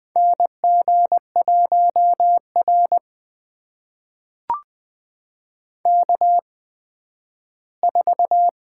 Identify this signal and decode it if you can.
Morse code, heard as a keyed tone, about 21 wpm, spaced as for 12 wpm, 700 Hz. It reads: NG1R K4
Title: USA Call Signs - 4 Characters - 8x Character Spacing - 20wpm